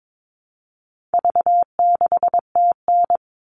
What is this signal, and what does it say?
Morse code, heard as a keyed tone, about 22 wpm, 700 Hz. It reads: V6TN
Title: All Cut Numbers - 4-Digit 22wpm